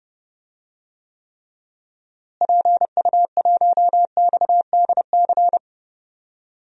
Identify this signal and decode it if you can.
Morse code, heard as a keyed tone, about 31 wpm, 700 Hz. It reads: PU1XDC
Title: DXCC-100 Call Signs - 6 Characters - Repeated 2x - 30wpm